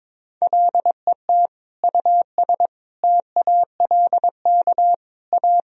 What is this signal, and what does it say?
Morse code, heard as a keyed tone, about 22 wpm, 700 Hz. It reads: LET US TALK A
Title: Sentences from Top 400 Words and Abbreviations 22wpm